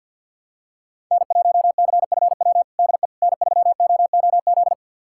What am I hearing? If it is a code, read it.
Morse code, heard as a keyed tone, about 53 wpm, 700 Hz. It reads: N1CFW DE N3OOZ
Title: USA Callsign DE Callsign 50wpm